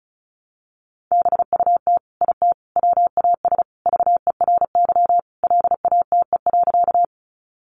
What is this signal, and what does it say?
Morse code, heard as a keyed tone, about 34 wpm, 700 Hz. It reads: BUT IT WAS VERY LATE.
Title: Sentences from Top 300 Words 35wpm